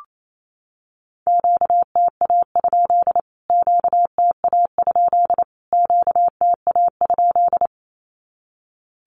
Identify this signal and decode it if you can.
Morse code, heard as a keyed tone, about 28 wpm, 700 Hz. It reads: QTA? QTA? QTA?
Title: Q-Codes Repeated 3x 28wpm